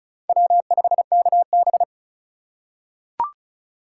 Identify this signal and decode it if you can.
Morse code, heard as a keyed tone, about 34 wpm, 700 Hz. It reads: W5KB
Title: USA Call Signs - 4x Character Spacing - 35wpm